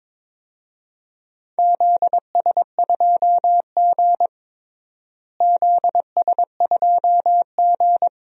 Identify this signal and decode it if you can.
Morse code, heard as a keyed tone, about 22 wpm, 700 Hz. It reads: ZS2G ZS2G
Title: DXCC-100 Call Signs - 4 Characters - Repeated 3x - 22wpm